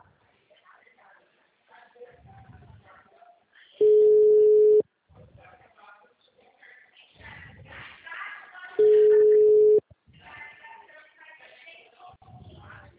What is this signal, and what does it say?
Ringback tone (Europe, 425 Hz), with speech.
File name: out- -2021- - - .421